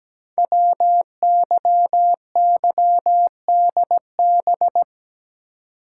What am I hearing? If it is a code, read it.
Morse code, heard as a keyed tone, about 17 wpm, 700 Hz. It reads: WYYDB
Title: Code Groups - 5 Letters - 17wpm